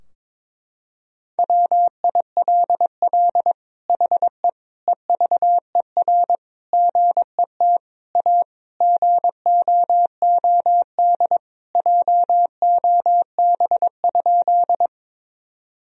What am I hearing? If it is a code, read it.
Morse code, heard as a keyed tone, about 22 wpm, 700 Hz. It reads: WILL HE EVER GET A GOOD JOB?
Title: Sentences from Top 1000 Words - Review Part 3 - 22wpm